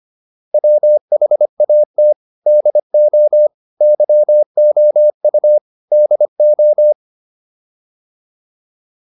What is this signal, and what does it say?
Morse code, heard as a keyed tone, about 25 wpm, 600 Hz. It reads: WHAT DO YOU DO